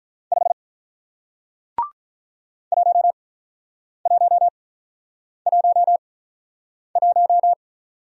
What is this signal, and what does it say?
Morse code, heard as a keyed tone, about 47 wpm, spaced as for 9 wpm, 700 Hz. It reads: 5 1111